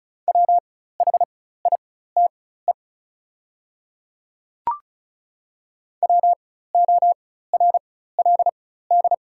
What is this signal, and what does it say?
Morse code, heard as a keyed tone, about 36 wpm, spaced as for 16 wpm, 700 Hz. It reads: WHITE WORLD
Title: 4x Character Spacing - Top 500 Words - 35wpm